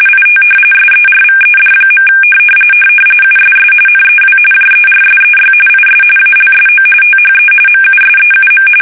RTTY, about 45.45 baud. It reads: IT SERVED. USS NEW JERSEY IS ALSO THE MUSEUM
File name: fsk850 / battleshipnj